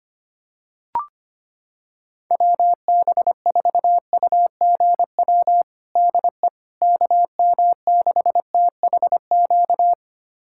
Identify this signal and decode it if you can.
Morse code, heard as a keyed tone, about 25 wpm, 700 Hz. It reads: WB4UGW DE KM6THQ
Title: USA Callsign DE Callsign - 6 Characters - 25wpm